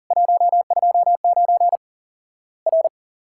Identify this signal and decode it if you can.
Morse code, heard as a keyed tone, about 40 wpm, 700 Hz. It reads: 129 R
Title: Worked All Europe DX Contest - QTC Exchange - 40wpm